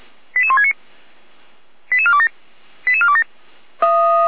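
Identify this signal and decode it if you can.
Five-tone selective call (ZVEI), heard as quick runs of five tones, tones of about 70 ms. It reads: 88169 88427 88427
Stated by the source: ZVEI five-tone selective call 88208 received via SDR.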